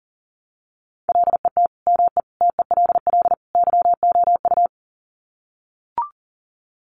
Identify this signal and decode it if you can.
Morse code, heard as a keyed tone, about 40 wpm, 700 Hz. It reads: LET ME TELL YOU
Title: Sets of 4 Words 40wpm